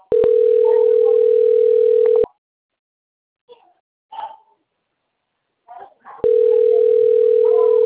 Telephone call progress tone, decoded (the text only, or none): ringback tone